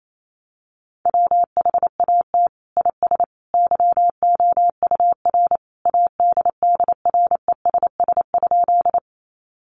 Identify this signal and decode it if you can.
Morse code, heard as a keyed tone, about 28 wpm, 700 Hz. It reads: WHAT IS YOUR ADDRESS?
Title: Sentences from Top 1000 Words 28wpm